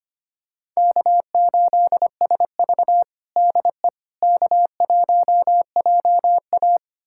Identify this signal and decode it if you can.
Morse code, heard as a keyed tone, about 25 wpm, 700 Hz. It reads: K8SV DE K1JA
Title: USA Callsign DE Callsign - 4 Characters - 25wpm